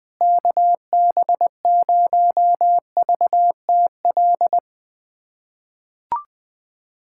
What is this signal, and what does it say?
Morse code, heard as a keyed tone, about 20 wpm, 700 Hz. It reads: KB0VTL
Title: USA Callsign DE Callsign 20wpm